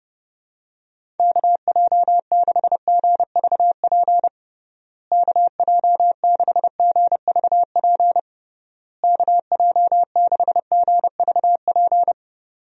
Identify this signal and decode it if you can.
Morse code, heard as a keyed tone, about 32 wpm, 700 Hz. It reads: KJ6GVP KJ6GVP KJ6GVP
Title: USA Call Signs - 6 Characters - Repeated 3x - 30wpm